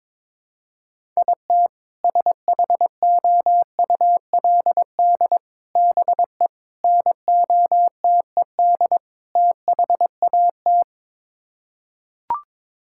Morse code, heard as a keyed tone, about 21 wpm, 700 Hz. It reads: IT SHOULD BE NOTED THAT
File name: practice / 22wpm / Build A Phrase - 5-Word Phrases - 22wpm